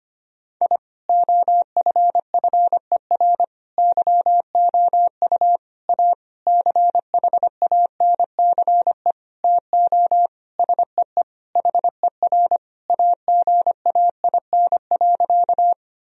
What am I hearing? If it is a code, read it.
Morse code, heard as a keyed tone, about 25 wpm, 700 Hz. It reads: I OFFER YOU A CHANCE TO SEE HER AGAIN.
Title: Sentences from Top 1000 Words - Review Part 4 - 25wpm